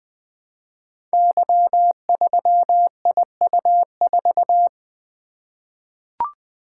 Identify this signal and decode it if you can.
Morse code, heard as a keyed tone, about 20 wpm, 700 Hz. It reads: Y3IU4
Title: Code Groups - 5 Letter:Numbers - 20wpm